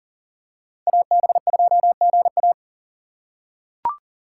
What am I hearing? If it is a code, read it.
Morse code, heard as a keyed tone, about 40 wpm, 700 Hz. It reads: AB2GA
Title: USA Call Signs - 5 Characters - 3x Character Spacing - 40wpm